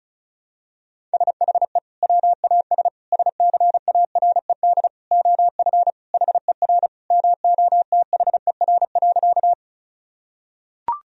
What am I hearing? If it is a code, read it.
Morse code, heard as a keyed tone, about 36 wpm, 700 Hz. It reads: SHE WAS SCARED OF HER MOTHER.
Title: Sentences from Top 1000 Words - Review Part 4 - 35wpm